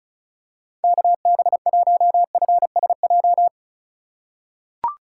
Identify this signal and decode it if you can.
Morse code, heard as a keyed tone, about 34 wpm, 700 Hz. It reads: KB1FSJ
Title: USA Call Signs - 2x Character Spacing - 35wpm